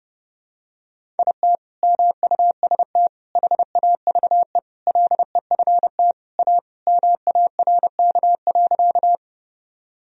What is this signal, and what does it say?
Morse code, heard as a keyed tone, about 32 wpm, 700 Hz. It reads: IT MUST HAVE LEFT A MARK.
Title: Sentences from Top 1000 Words - Review Part 1 - 30wpm